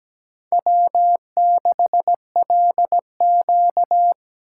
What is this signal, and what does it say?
Morse code, heard as a keyed tone, about 17 wpm, 700 Hz. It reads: W6LQ
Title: USA Call Signs - 4 Characters - 6x Character Spacing - 17wpm